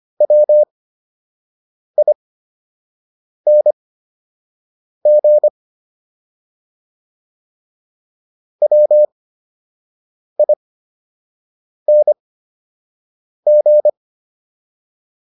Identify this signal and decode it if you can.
Morse code, heard as a keyed tone, about 25 wpm, spaced as for 6 wpm, 600 Hz. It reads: WING WING